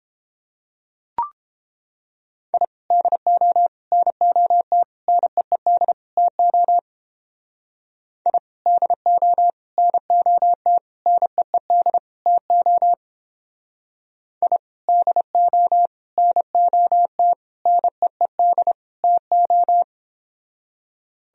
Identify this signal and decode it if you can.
Morse code, heard as a keyed tone, about 29 wpm, 700 Hz. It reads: I DO NOT NEED TO I DO NOT NEED TO I DO NOT NEED TO